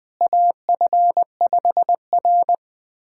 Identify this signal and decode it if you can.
Morse code, heard as a keyed tone, about 20 wpm, 700 Hz. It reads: AF5R
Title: USA Callsign DE Callsign - 4 Characters - 20wpm